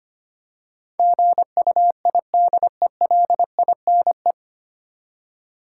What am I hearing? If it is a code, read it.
Morse code, heard as a keyed tone, about 25 wpm, 700 Hz. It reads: GUIDELINE
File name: practice / 25wpm / 9 Letter Words - First 3 Letters Given - 25wpm